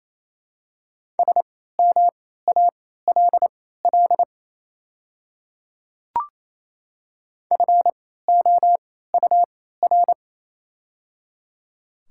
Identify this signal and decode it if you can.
Morse code, heard as a keyed tone, about 27 wpm, spaced as for 16 wpm, 700 Hz. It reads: SMALL FOUR
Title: 3x Character Spacing - Top 300 Words - 28wpm